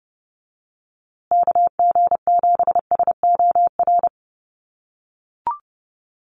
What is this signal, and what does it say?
Morse code, heard as a keyed tone, about 30 wpm, 700 Hz. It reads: KG7SOR
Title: USA Call Signs Repeated 2x 30wpm